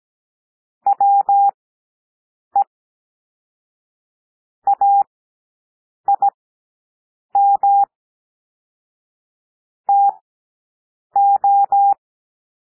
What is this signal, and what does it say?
Morse code, heard as a keyed tone, about 17 wpm, spaced as for 7 wpm, 800 Hz. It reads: WE AIM TO
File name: SOUND PLAIN 17 wpm